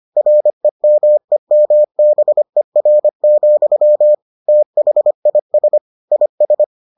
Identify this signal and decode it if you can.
Morse code, heard as a keyed tone, about 25 wpm, 600 Hz. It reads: REMEMBER, THIS IS